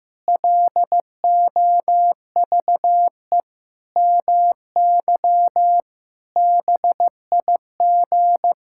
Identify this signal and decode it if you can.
Morse code, heard as a keyed tone, about 15 wpm, 700 Hz. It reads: LOVE MY BIG
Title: Pangram Challenge v2 - 15wpm